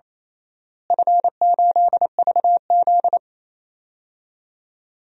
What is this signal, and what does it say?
Morse code, heard as a keyed tone, about 27 wpm, 700 Hz. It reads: F8VZ
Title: DXCC-100 Call Signs - 4 Characters - Repeated 2x - 28wpm